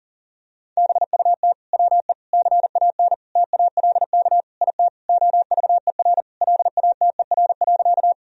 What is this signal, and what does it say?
Morse code, heard as a keyed tone, about 40 wpm, 700 Hz. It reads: BUT WE CAN TALK IT OVER LATER.
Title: Sentences from Top 1000 Words - Review Part 4 - 40wpm